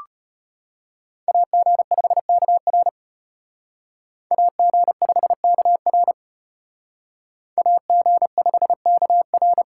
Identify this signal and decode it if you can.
Morse code, heard as a keyed tone, about 33 wpm, 700 Hz. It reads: AG5KR AG5KR AG5KR